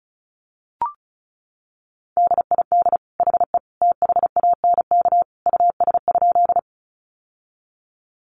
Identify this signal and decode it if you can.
Morse code, heard as a keyed tone, about 34 wpm, 700 Hz. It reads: DID HE THANK US?
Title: Sentences from Top 600 Words 35wpm